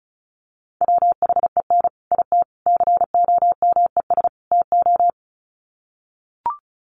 Morse code, heard as a keyed tone, about 34 wpm, 700 Hz. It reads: WHEN IT COMES TO
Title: Sets of 4 Words 35wpm